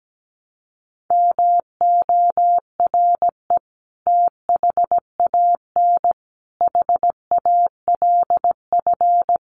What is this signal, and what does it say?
Morse code, heard as a keyed tone, about 17 wpm, 700 Hz. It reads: MORE THAN HALF
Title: Sets of 5 Words 17wpm